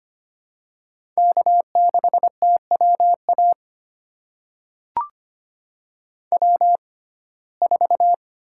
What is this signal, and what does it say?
Morse code, heard as a keyed tone, about 25 wpm, spaced as for 18 wpm, 700 Hz. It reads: K6TWA W4
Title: USA Call Signs - 5 Characters - 6x Character Spacing - 25wpm